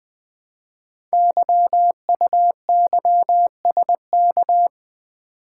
Morse code, heard as a keyed tone, about 20 wpm, 700 Hz. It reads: YUYSK
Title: Code Groups - 5 Letters - 20wpm